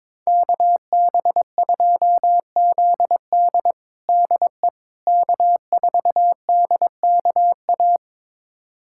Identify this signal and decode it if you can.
Morse code, heard as a keyed tone, about 21 wpm, 700 Hz. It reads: KB2ZD DE K4DKA
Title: USA Callsign DE Callsign - 5 Characters - 22wpm